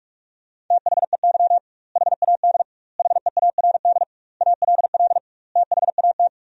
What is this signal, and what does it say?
Morse code, heard as a keyed tone, about 46 wpm, 700 Hz. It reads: THEY HAD HEARD ALL THAT
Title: 2x Word-Spacing - Sentences from Top 400 Words - 45wpm